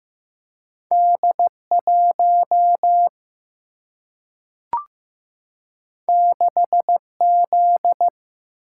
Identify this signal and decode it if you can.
Morse code, heard as a keyed tone, about 15 wpm, 700 Hz. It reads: D 1 6 Z
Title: Single-Letter Single-Number 15wpm